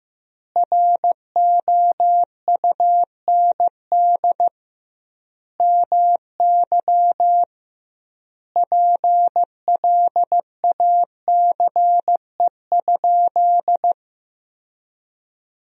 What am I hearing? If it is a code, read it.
Morse code, heard as a keyed tone, about 15 wpm, 700 Hz. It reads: ROUND MY PLACE?
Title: 2x Word-Spacing - Sentences from Top 200 Words - 15wpm